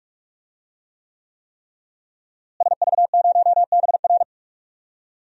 Morse code, heard as a keyed tone, about 46 wpm, 700 Hz. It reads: SV0BR
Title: DXCC-100 Call Signs - 5 Characters - Repeated 3x - 45wpm